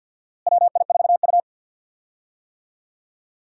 Morse code, heard as a keyed tone, about 46 wpm, 700 Hz. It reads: WI4U
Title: USA Call Signs - 4 Characters - Repeated 3x - 50wpm